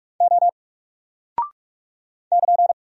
Morse code, heard as a keyed tone, about 46 wpm, 700 Hz. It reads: O <KN>
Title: Single Character (Letter:Number:Punctuation:Prosign) Alt 45wpm